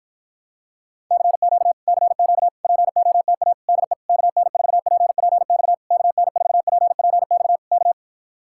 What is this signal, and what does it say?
Morse code, heard as a keyed tone, about 52 wpm, 700 Hz. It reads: CQ CQ POTA DE KN4PPX KN4PPX K